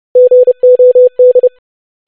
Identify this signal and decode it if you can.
Morse code, heard as a keyed tone, about 30 wpm, 500 Hz. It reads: GOD
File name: basic500-25wpm